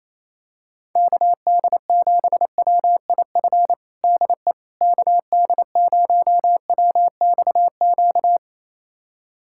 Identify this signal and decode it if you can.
Morse code, heard as a keyed tone, about 28 wpm, 700 Hz. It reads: KD7WIF DE KD0WXQ
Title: USA Callsign DE Callsign - 6 Characters - 28wpm